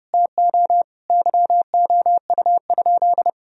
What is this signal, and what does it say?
Morse code, heard as a keyed tone, about 30 wpm, 700 Hz. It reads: TO YOU?
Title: Sentences from Top 1000 Words - Review Part 1 - 30wpm